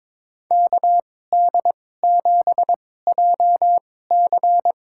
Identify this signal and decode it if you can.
Morse code, heard as a keyed tone, about 22 wpm, spaced as for 16 wpm, 700 Hz. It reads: KD7JC
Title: USA Call Signs - 2x Character Spacing - 22wpm